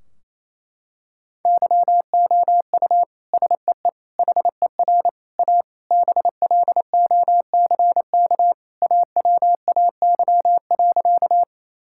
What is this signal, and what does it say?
Morse code, heard as a keyed tone, about 28 wpm, 700 Hz. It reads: YOU SEE HER A BLOCK AWAY.
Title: Sentences from Top 1600 Words 28wpm